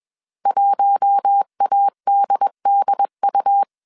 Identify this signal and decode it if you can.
Morse code, heard as a keyed tone, about 21 wpm, 800 Hz. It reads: 1ADDU
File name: CW / DET / 1 6 1500C S A 20wpm 2048